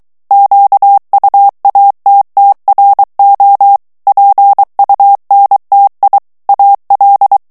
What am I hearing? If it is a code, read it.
Morse code, heard as a keyed tone, about 23 wpm, 800 Hz. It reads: QUATTRO PUNTI AL
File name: ITALIEN texte clair 2 vitesse 22 mots minute